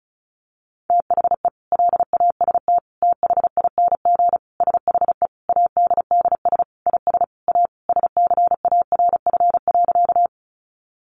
Morse code, heard as a keyed tone, about 34 wpm, 700 Hz. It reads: THE LAST THING SHE ADDS IS A SCARF.
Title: Easy Phrases 35wpm